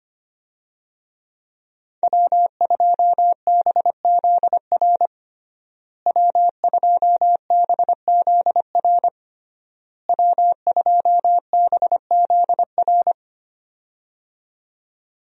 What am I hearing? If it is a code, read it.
Morse code, heard as a keyed tone, about 25 wpm, 700 Hz. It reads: W2BZR W2BZR W2BZR
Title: USA Call Signs - 5 Characters - Repeated 3x - 25wpm